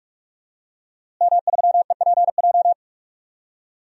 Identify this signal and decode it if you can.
Morse code, heard as a keyed tone, about 46 wpm, 700 Hz. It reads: M3EPJ
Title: DXCC-100 Call Signs - 5 Characters - Repeated 2x - 45wpm